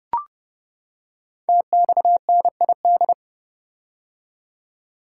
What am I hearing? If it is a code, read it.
Morse code, heard as a keyed tone, about 30 wpm, 700 Hz. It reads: TXNID
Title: Code Groups - 5 Letter:Numbers - 30wpm